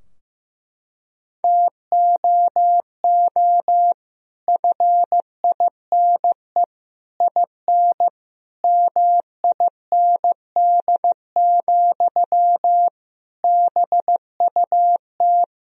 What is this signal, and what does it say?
Morse code, heard as a keyed tone, about 15 wpm, 700 Hz. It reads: TOO FINE IN MIND, BUT
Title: Sentences from Top 1000 Words - Review Part 4 - 15wpm